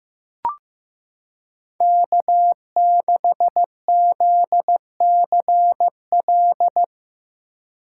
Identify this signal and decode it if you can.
Morse code, heard as a keyed tone, about 15 wpm, 700 Hz. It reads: K6ZCL
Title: K1USN SST QSO Practice - 15wpm